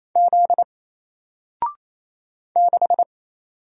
Morse code, heard as a keyed tone, about 28 wpm, 700 Hz. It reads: Z 6
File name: Single-Letter-Number-28wpm